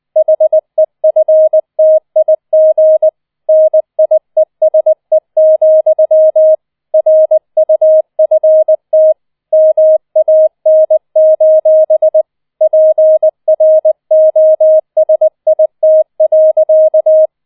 Morse code, heard as a keyed tone, about 20 wpm, 630 Hz. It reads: HEFTIG NIESE, RUFT MAN: PROSIT.
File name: 20wpm